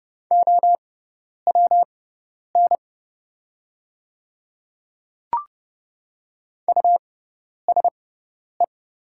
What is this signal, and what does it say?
Morse code, heard as a keyed tone, about 30 wpm, spaced as for 10 wpm, 700 Hz. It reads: OWN USE
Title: 6x Character Spacing - Top 300 Words - 30wpm